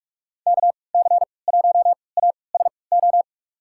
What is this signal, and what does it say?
Morse code, heard as a keyed tone, about 46 wpm, spaced as for 25 wpm, 700 Hz. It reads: KC1ASO
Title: USA Call Signs - 6 Characters - 3x Character Spacing - 45wpm